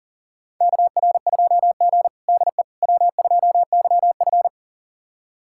Morse code, heard as a keyed tone, about 40 wpm, 700 Hz. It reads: KR2G DE W2YF
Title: USA Callsign DE Callsign - 4 Characters - 40wpm